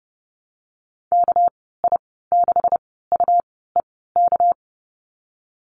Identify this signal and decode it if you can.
Morse code, heard as a keyed tone, about 30 wpm, spaced as for 17 wpm, 700 Hz. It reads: KI6UEK
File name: Lesson 37g - Letter x - Call Signs 30wpm